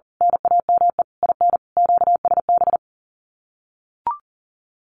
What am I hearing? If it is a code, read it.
Morse code, heard as a keyed tone, about 40 wpm, 700 Hz. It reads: NAME IN QSB
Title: Common QSO Phrases 40wpm